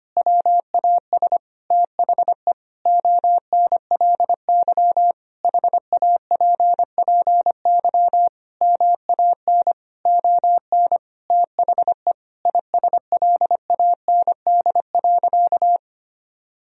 Morse code, heard as a keyed tone, about 25 wpm, 700 Hz. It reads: WAS THE ONLY HAPPY MAN ON THE ISLAND.
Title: Sentences from Top 1000 Words - Review Part 3 - 25wpm